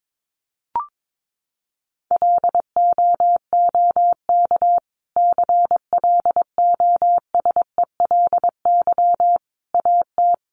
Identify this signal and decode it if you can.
Morse code, heard as a keyed tone, about 22 wpm, 700 Hz. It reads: LOOK CLOSELY AT
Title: Sentences from Top 1400 Words 22wpm